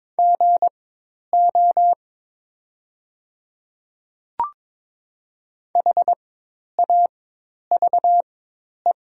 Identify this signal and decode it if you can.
Morse code, heard as a keyed tone, about 22 wpm, spaced as for 10 wpm, 700 Hz. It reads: GO HAVE
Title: 4x Character Spacing - Top 200 Words - 22wpm